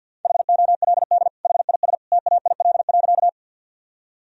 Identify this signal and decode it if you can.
Morse code, heard as a keyed tone, about 50 wpm, 700 Hz. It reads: HOLD HIS TAIL.
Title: Sentences from Top 1000 Words - Review Part 3 - 50wpm